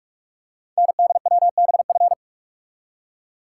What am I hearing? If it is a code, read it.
Morse code, heard as a keyed tone, about 44 wpm, 700 Hz. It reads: NDWBF
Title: Code Groups - 5 Letters - 45wpm